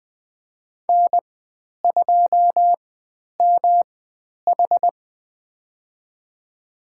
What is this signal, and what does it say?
Morse code, heard as a keyed tone, about 20 wpm, spaced as for 10 wpm, 700 Hz. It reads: N2MH
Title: 109 Lesson 18g - Letter m - Call Signs 20wpm